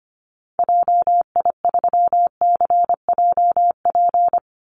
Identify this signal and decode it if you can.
Morse code, heard as a keyed tone, about 24 wpm, 700 Hz. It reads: JI3CJP
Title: Call Signs Repeated 2x - 25wpm